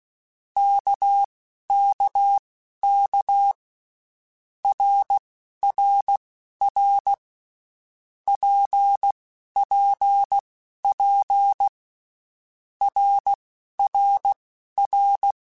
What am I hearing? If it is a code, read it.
Morse code, heard as a keyed tone, about 16 wpm, spaced as for 12 wpm, 800 Hz. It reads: KKK RRR PPP RRR